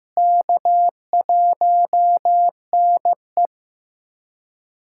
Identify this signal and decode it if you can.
Morse code, heard as a keyed tone, about 15 wpm, 700 Hz. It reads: K1NE
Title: USA Callsign DE Callsign - 4 Characters - 15wpm